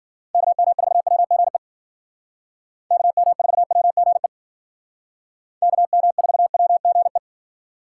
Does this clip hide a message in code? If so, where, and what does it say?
Yes: Morse code from 0.3 to 7.2 s, heard as a keyed tone, about 51 wpm, 700 Hz. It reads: KM4WGE KM4WGE KM4WGE